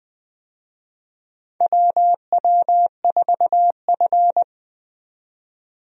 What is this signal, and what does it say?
Morse code, heard as a keyed tone, about 20 wpm, 700 Hz. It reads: WW4F
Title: USA Call Signs - 4 Characters - 20wpm